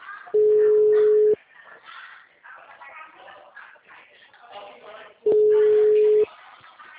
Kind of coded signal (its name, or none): telephone call progress tone